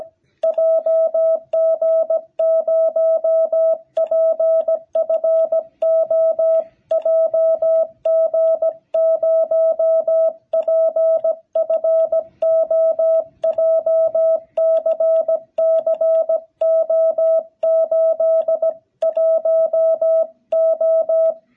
Morse code, heard as a keyed tone, about 17 wpm, 650 Hz. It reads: JG0PFO JG0PFO JCC O81O
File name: jh8cbh / 2021cw